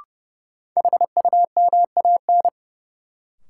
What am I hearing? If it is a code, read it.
Morse code, heard as a keyed tone, about 30 wpm, 700 Hz. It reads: HUMAN
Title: Sentences from Top 600 Words - New Word Drill - 30wpm